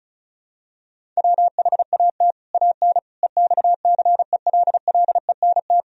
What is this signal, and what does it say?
Morse code, heard as a keyed tone, about 34 wpm, 700 Hz. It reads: WHAT AN EXCELLENT
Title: Sentences from Top 1000 Words - Review Part 1 - 35wpm